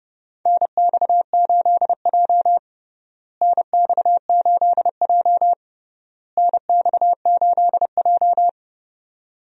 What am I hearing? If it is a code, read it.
Morse code, heard as a keyed tone, about 30 wpm, 700 Hz. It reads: NX8J NX8J NX8J
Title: USA Call Signs - 4 Characters - Repeated 3x - 30wpm